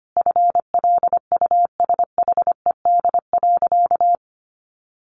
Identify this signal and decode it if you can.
Morse code, heard as a keyed tone, about 25 wpm, 700 Hz. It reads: FLUSHED.
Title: A Sound Of Thunder 25wpm